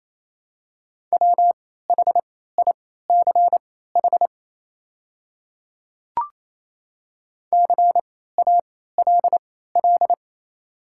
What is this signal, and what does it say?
Morse code, heard as a keyed tone, about 28 wpm, spaced as for 16 wpm, 700 Hz. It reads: WHICH CALL
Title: 3x Character Spacing - Top 200 Words - 28wpm